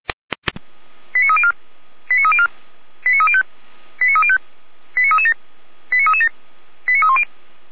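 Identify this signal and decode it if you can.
Five-tone selective call (ZVEI), heard as quick runs of five tones, tones of about 70 ms. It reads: 89304 89304 89305 89305 89337 89337 89310
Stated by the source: ZVEI five-tone selective call received on an SDR.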